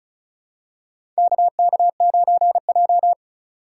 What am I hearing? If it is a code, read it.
Morse code, heard as a keyed tone, about 36 wpm, 700 Hz. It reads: KK9J
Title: USA Call Signs - 4 Characters - 3x Character Spacing - 35wpm